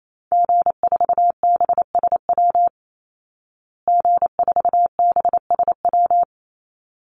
Morse code, heard as a keyed tone, about 28 wpm, 700 Hz. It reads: G4BSW G4BSW
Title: Call Signs Repeated 3x - 28wpm